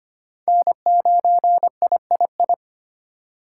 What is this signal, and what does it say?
Morse code, heard as a keyed tone, about 25 wpm, 700 Hz. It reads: N9III
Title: USA Call Signs - 3x Character Spacing - 25wpm